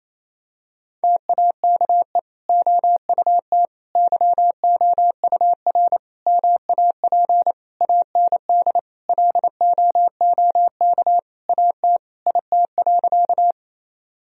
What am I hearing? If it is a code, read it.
Morse code, heard as a keyed tone, about 28 wpm, 700 Hz. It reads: TAKE OUT YOUR MAP AND LOOK AT IT.
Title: Sentences from Top 1000 Words - Review Part 2 - 28wpm